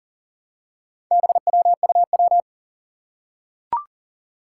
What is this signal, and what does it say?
Morse code, heard as a keyed tone, about 40 wpm, 700 Hz. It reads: BWUW
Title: Four Challenging Letter-Number Combinations 40wpm